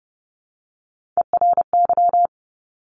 Morse code, heard as a keyed tone, about 30 wpm, 700 Hz. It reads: ERY
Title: Top Three Letter Combinations in Words 30wpm